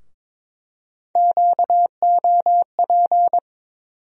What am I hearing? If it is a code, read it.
Morse code, heard as a keyed tone, about 22 wpm, 700 Hz. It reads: QOP
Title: USA Call Sign Postfixes - 3 Letters 22wpm